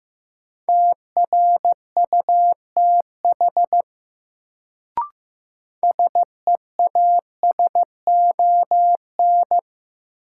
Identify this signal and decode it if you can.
Morse code, heard as a keyed tone, about 15 wpm, 700 Hz. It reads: TRUTH SEASON
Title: Sets of 5 Words - New Word Drill - 15wpm